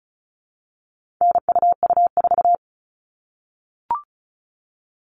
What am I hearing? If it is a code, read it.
Morse code, heard as a keyed tone, about 34 wpm, 700 Hz. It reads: NUU4
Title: All Cut Numbers - 4-Digit 35wpm